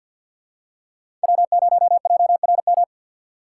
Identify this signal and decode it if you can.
Morse code, heard as a keyed tone, about 47 wpm, 700 Hz. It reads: W0JRM
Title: USA Call Signs - 5 Characters - 6x Character Spacing - 50wpm